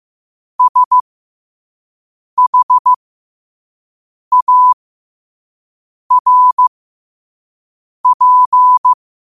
Morse code, heard as a keyed tone, about 15 wpm, spaced as for 5 wpm, 1000 Hz. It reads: SHARP
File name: qotdfiles / 5wpm